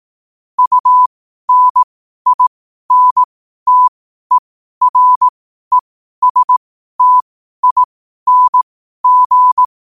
Morse code, heard as a keyed tone, about 18 wpm, spaced as for 12 wpm, 1000 Hz. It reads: UNINTERESTING
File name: qotdfiles / 12wpm